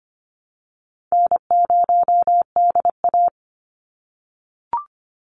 Morse code, heard as a keyed tone, about 25 wpm, 700 Hz. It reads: N0DA
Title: NA SOTA Chaser Callsigns 25wpm